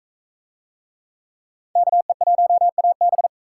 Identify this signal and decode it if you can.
Morse code, heard as a keyed tone, about 38 wpm, 700 Hz. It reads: KE1AB